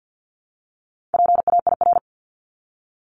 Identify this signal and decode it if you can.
Morse code, heard as a keyed tone, about 50 wpm, 700 Hz. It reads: PAIR
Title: Top 1001-1250 Words 50wpm